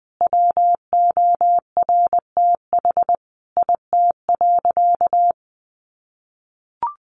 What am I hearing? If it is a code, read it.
Morse code, heard as a keyed tone, about 20 wpm, 700 Hz. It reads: WORTH IT.
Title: Sentences from Top 1300 Words 20wpm